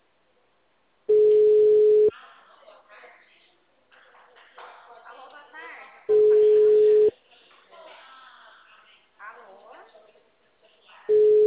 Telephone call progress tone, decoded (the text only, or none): ringback tone